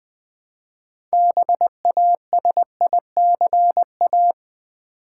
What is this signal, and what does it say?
Morse code, heard as a keyed tone, about 20 wpm, 700 Hz. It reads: BASICA
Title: Build A Word - 3 Letters At A Time - 9-Letter Words - 20wpm